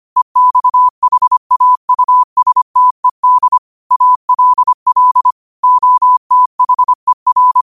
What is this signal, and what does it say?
Morse code, heard as a keyed tone, about 25 wpm, 1000 Hz. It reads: EXHAUSTED ALL OTHER
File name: qotdfiles / 25wpm